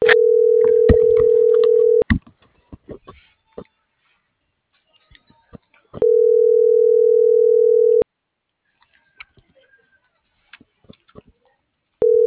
Ringback tone (North America), 440 + 480 Hz.